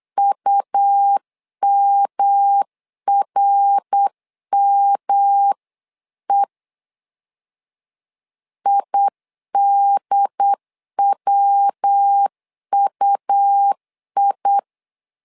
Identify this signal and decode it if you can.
Morse code, heard as a keyed tone, about 8 wpm, 800 Hz. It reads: UMRME IDWUI
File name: LEZIONI / 3 MWRO / 3 MWR 8wpm 2444